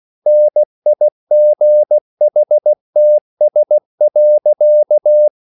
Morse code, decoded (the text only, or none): NIGHTS.